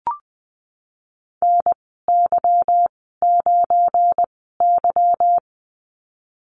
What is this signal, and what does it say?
Morse code, heard as a keyed tone, about 20 wpm, spaced as for 15 wpm, 700 Hz. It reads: NY9Y
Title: 179 Lesson 29g - Slash - Call Signs 20wpm